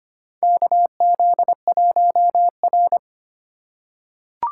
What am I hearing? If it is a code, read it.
Morse code, heard as a keyed tone, about 25 wpm, 700 Hz. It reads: KZ1R
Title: USA Call Signs - 4x Character Spacing - 25wpm